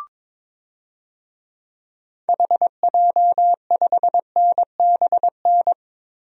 Morse code, heard as a keyed tone, about 22 wpm, 700 Hz. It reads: HJ5NBN
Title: DXCC-100 Call Signs - 6 Characters - Repeated 2x - 22wpm